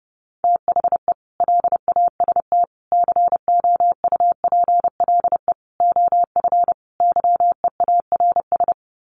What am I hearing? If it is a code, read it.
Morse code, heard as a keyed tone, about 28 wpm, 700 Hz. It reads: THE LAST COUPLE OF YEARS
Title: Sets of 5 Words 30wpm